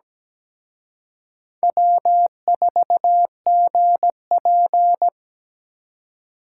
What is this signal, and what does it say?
Morse code, heard as a keyed tone, about 20 wpm, 700 Hz. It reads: W4GP
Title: Speed Racing - USA 1x2 Call Signs 17wpm